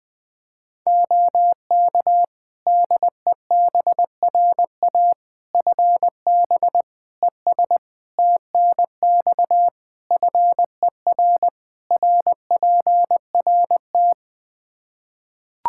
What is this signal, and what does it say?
Morse code, heard as a keyed tone, about 20 wpm, 700 Hz. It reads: OK DEBRA FB ES TNX FER RPRT
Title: Practice QSO - Exchange 2 - Info - 20wpm